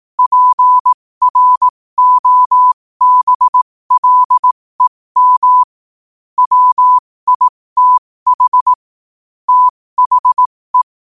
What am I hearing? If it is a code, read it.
Morse code, heard as a keyed tone, about 18 wpm, spaced as for 15 wpm, 1000 Hz. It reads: PROBLEM WITH THE
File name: 15wpm